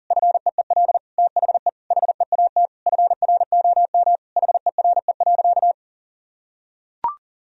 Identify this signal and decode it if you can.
Morse code, heard as a keyed tone, about 40 wpm, 700 Hz. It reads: FEEL THE HEAT FROM HERE.
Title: Sentences from Top 1000 Words - Review Part 3 - 40wpm